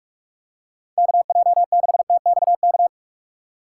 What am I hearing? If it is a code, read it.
Morse code, heard as a keyed tone, about 42 wpm, 700 Hz. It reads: KJ6TXK
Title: USA Call Signs - 6 Characters - 3x Character Spacing - 45wpm